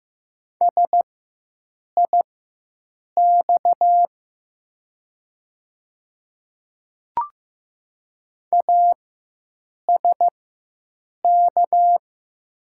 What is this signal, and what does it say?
Morse code, heard as a keyed tone, about 15 wpm, spaced as for 7 wpm, 700 Hz. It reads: SIX ASK
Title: 4x Character Spacing - Top 500 Words - 15wpm